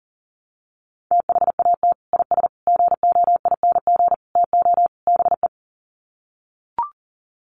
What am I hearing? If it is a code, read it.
Morse code, heard as a keyed tone, about 37 wpm, 700 Hz. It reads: THAT IS GOING TO BE
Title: Sets of 5 Words Encore2 40wpm